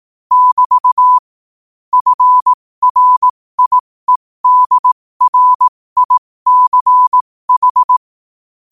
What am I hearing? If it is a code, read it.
Morse code, heard as a keyed tone, about 18 wpm, spaced as for 15 wpm, 1000 Hz. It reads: <BT> FRIEDRICH
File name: qotdfiles / 15wpm